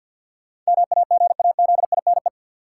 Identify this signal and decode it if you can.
Morse code, heard as a keyed tone, about 53 wpm, 700 Hz. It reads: MAGAZINE
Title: Sentences from Top 1000 Words - Review Part 7 - New Word Drill - 50wpm